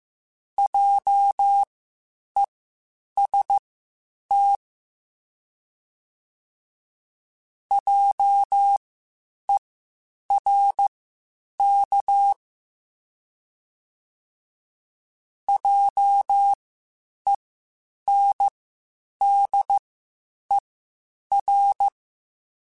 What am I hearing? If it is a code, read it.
Morse code, heard as a keyed tone, about 15 wpm, spaced as for 8 wpm, 800 Hz. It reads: JEST JERK JENDER
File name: Spaced Alphas / Spaced Alphas mp3s / Js sp @13